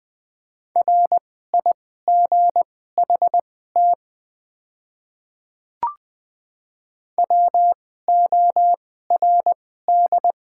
Morse code, heard as a keyed tone, about 20 wpm, spaced as for 14 wpm, 700 Hz. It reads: RIGHT WORD
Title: 2x Character Spacing - Top 200 Words - 20wpm